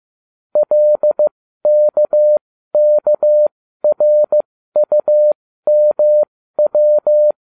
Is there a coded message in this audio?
Yes: Morse code from 0.6 to 7.3 s, heard as a keyed tone, about 15 wpm, spaced as for 13 wpm, 600 Hz. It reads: LKKRUMW